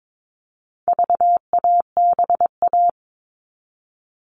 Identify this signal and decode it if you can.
Morse code, heard as a keyed tone, about 22 wpm, 700 Hz. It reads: VABA
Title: All Cut Numbers - 4-Digit 22wpm